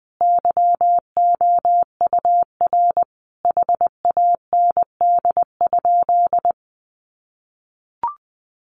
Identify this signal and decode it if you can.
Morse code, heard as a keyed tone, about 20 wpm, 700 Hz. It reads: YOUR HAND?
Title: Sentences from Top 200 Words 20wpm